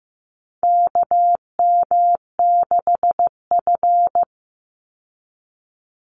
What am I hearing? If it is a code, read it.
Morse code, heard as a keyed tone, about 15 wpm, 700 Hz. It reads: KM6F
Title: USA Call Signs 15wpm